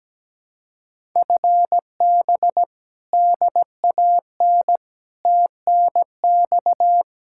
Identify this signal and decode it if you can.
Morse code, heard as a keyed tone, about 17 wpm, 700 Hz. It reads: FB DAN TNX
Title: Speed-Racing Common QSO Phrases - 17wpm